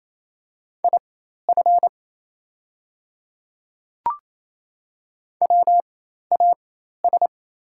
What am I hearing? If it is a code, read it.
Morse code, heard as a keyed tone, about 28 wpm, spaced as for 13 wpm, 700 Hz. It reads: IF WAS
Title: 4x Character Spacing - Top 100 Words - 28wpm